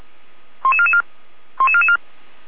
Five-tone selective call (ZVEI), heard as quick runs of five tones, tones of about 70 ms. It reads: 11444 11444
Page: 455/SDR/ZVEI/88311